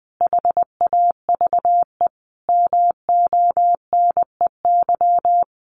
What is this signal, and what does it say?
Morse code, heard as a keyed tone, about 20 wpm, 700 Hz. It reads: HAVE MONEY
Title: Sentences from Top 1400 Words 20wpm